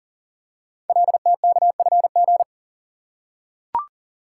Morse code, heard as a keyed tone, about 42 wpm, 700 Hz. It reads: LTKFG
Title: Code Groups - 5 Letters - 40wpm